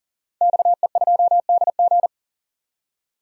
Morse code, heard as a keyed tone, about 40 wpm, 700 Hz. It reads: XE2DG
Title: DXCC-100 Call Signs - 5 Characters - 40wpm